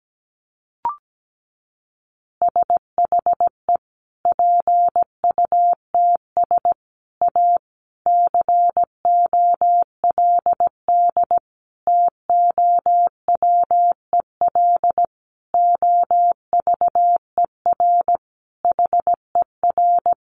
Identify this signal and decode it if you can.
Morse code, heard as a keyed tone, about 17 wpm, 700 Hz. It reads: SHE PUTS A COLD TOWEL OVER HER
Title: Easy Phrases 17wpm